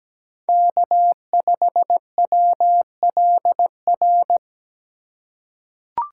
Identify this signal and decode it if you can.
Morse code, heard as a keyed tone, about 17 wpm, 700 Hz. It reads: K5WLR
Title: USA Call Signs - 5 Characters - 3x Character Spacing - 17wpm